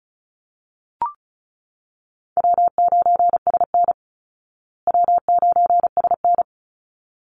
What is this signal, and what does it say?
Morse code, heard as a keyed tone, about 34 wpm, 700 Hz. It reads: W9SN W9SN
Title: Easy Call Signs Repeated 2x - 35wpm